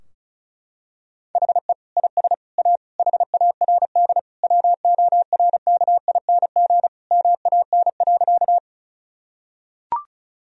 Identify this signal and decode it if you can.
Morse code, heard as a keyed tone, about 36 wpm, 700 Hz. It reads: HE IS A HARD WORKING MAN.
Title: Sentences from Top 1000 Words - Review Part 4 - 35wpm